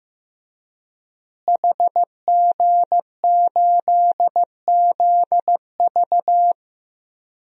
Morse code, heard as a keyed tone, about 15 wpm, 700 Hz. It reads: HG8ZV
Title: DXCC-100 Call Signs - 5 Characters - Repeated 3x - 15wpm